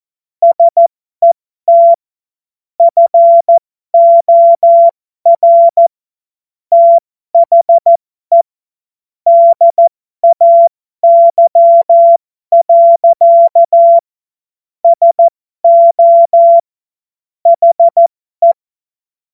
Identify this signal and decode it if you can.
Morse code, heard as a keyed tone, about 14 wpm, spaced as for 12 wpm, 680 Hz. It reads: SET FOR THE DAY. SO HE